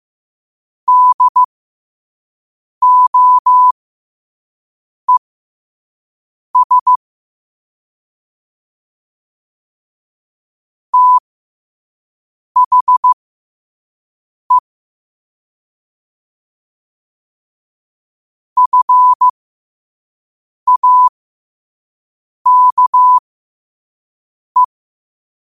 Morse code, heard as a keyed tone, about 15 wpm, spaced as for 5 wpm, 1000 Hz. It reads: DOES THE FAKE
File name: qotdfiles / 5wpm